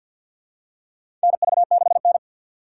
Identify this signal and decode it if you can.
Morse code, heard as a keyed tone, about 46 wpm, 700 Hz. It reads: NV6N
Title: USA Call Signs - 4 Characters - 6x Character Spacing - 50wpm